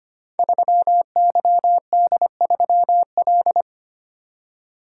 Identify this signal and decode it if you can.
Morse code, heard as a keyed tone, about 25 wpm, 700 Hz. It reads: 3YD3L
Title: Code Groups - 5 Letter:Numbers - 25wpm